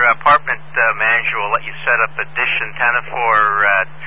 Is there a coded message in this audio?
Yes: Emergency Alert System (SAME) tones from 0.5 to 3.1 s, heard as harsh buzzing bursts, with speech.